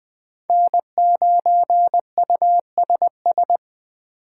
Morse code, heard as a keyed tone, about 20 wpm, 700 Hz. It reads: N9USS